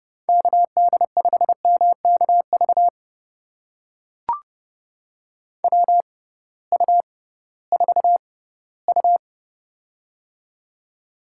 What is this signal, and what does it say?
Morse code, heard as a keyed tone, about 30 wpm, spaced as for 18 wpm, 700 Hz. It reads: KD5MKV WU4U
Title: USA Call Signs - 6x Character Spacing - 30wpm